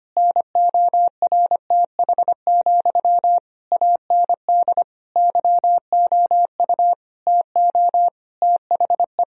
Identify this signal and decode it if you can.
Morse code, heard as a keyed tone, about 25 wpm, 700 Hz. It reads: NORTH, AND YOU TO THE
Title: Sentences from Top 1000 Words - Review Part 2 - 25wpm